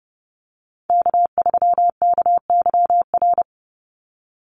Morse code, heard as a keyed tone, about 30 wpm, 700 Hz. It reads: K3KYR
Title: NA SOTA Chaser Callsigns 30wpm